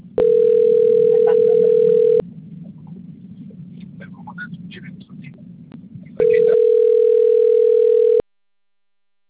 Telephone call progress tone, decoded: ringback tone

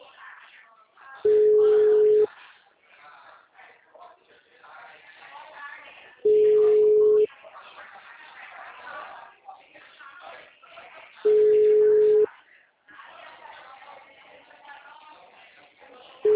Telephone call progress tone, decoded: ringback tone

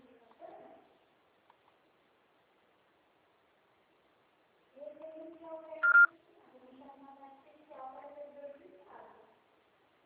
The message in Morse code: none

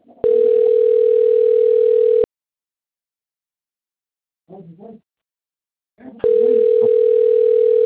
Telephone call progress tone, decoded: ringback tone

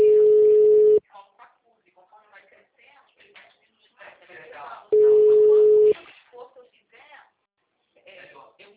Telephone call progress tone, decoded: ringback tone

